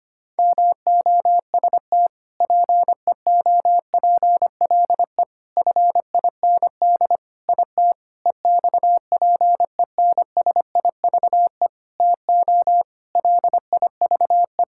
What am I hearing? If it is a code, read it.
Morse code, heard as a keyed tone, about 25 wpm, 700 Hz. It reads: MOST PEOPLE FIND IT EXPENSIVE TO LIVE
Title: Sentences from Top 1000 Words - Review Part 6 - 25wpm